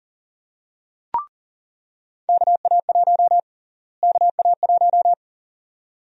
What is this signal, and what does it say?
Morse code, heard as a keyed tone, about 40 wpm, 700 Hz. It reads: KA1 KA1
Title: USA Call Sign Prefixes - Repeated 2x - 40wpm